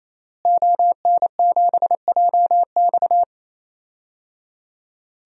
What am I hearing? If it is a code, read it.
Morse code, heard as a keyed tone, about 27 wpm, 700 Hz. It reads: ON7JX
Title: DXCC-100 Call Signs - 5 Characters - Repeated 3x - 28wpm